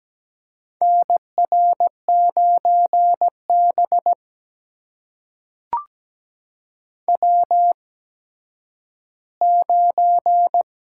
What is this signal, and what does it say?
Morse code, heard as a keyed tone, about 17 wpm, spaced as for 10 wpm, 700 Hz. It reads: NR9B W9
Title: USA Call Signs - 4 Characters - 8x Character Spacing - 17wpm